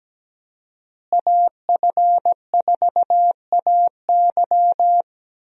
Morse code, heard as a keyed tone, about 17 wpm, 700 Hz. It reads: AF4AY
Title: USA Call Signs - 5 Characters - Repeated 3x - 17wpm